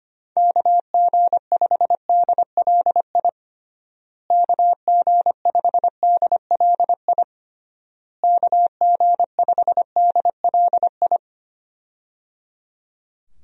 Morse code, heard as a keyed tone, about 25 wpm, 700 Hz. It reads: KG5DLI KG5DLI KG5DLI
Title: USA Call Signs - 6 Characters - Repeated 3x - 25wpm